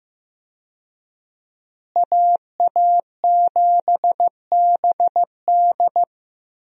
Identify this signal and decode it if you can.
Morse code, heard as a keyed tone, about 15 wpm, 700 Hz. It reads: AA7BD
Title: USA Call Signs - 5 Characters - Repeated 2x - 15wpm